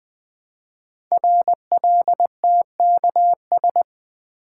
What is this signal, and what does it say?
Morse code, heard as a keyed tone, about 20 wpm, 700 Hz. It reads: RLTKS
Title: Code Groups - 5 Letters - 20wpm